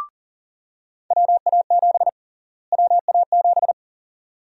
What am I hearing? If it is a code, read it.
Morse code, heard as a keyed tone, about 38 wpm, 700 Hz. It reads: WA7 WA7
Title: USA Call Sign Prefixes - Repeated 2x - 40wpm